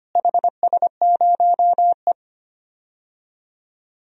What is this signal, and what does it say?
Morse code, heard as a keyed tone, about 25 wpm, 700 Hz. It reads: HS0E
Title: DXCC-100 Call Signs - 4 Characters - Repeated 3x - 25wpm